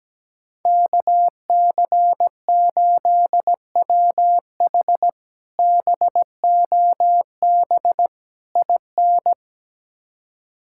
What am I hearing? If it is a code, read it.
Morse code, heard as a keyed tone, about 17 wpm, 700 Hz. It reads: KC8WH BOB IN
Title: K1USN SST QSO Practice - 17wpm